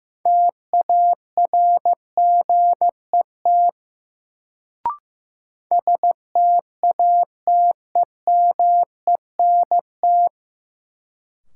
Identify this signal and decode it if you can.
Morse code, heard as a keyed tone, about 15 wpm, 700 Hz. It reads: TARGET STATEMENT
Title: Sentences from Top 1200 Words - New Word Drill - 15wpm